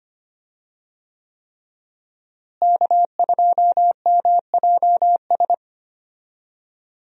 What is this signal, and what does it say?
Morse code, heard as a keyed tone, about 25 wpm, 700 Hz. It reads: K2MJS
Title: USA Call Signs - 5 Characters - Repeated 3x - 25wpm